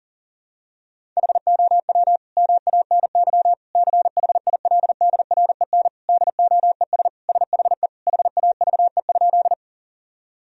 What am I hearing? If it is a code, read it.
Morse code, heard as a keyed tone, about 37 wpm, 700 Hz. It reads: HOW MANY CHILDREN DOES SHE HAVE?
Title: Sentences from Top 1000 Words - Review Part 1 - 40wpm